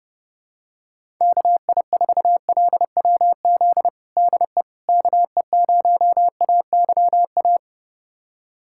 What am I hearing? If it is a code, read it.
Morse code, heard as a keyed tone, about 32 wpm, 700 Hz. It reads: KI4LWZ DE KE0AYA
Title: USA Callsign DE Callsign - 6 Characters - 30wpm